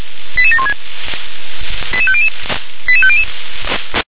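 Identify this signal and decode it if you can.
Five-tone selective call (ZVEI), heard as quick runs of five tones, tones of about 70 ms. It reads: 88716 88500 88500
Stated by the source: ZVEI five-tone selective call received via SDR.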